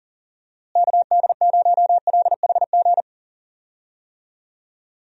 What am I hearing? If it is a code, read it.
Morse code, heard as a keyed tone, about 40 wpm, 700 Hz. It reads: KD0LHG
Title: USA Call Signs - 6 Characters - Repeated 2x - 40wpm